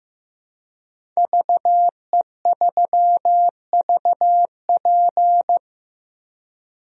Morse code, heard as a keyed tone, about 15 wpm, 700 Hz. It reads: VE3VP
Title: DXCC-100 Call Signs - 5 Characters - Repeated 2x - 15wpm